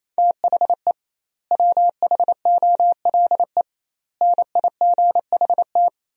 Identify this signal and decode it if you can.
Morse code, heard as a keyed tone, about 28 wpm, 700 Hz. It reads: THE WHOLE NIGHT
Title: 2x Word-Spacing - Sentences from Top 500 Words - 28wpm